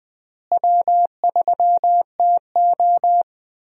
Morse code, heard as a keyed tone, about 20 wpm, 700 Hz. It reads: W3TO
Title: USA Call Signs - 4 Characters - Repeated 3x - 20wpm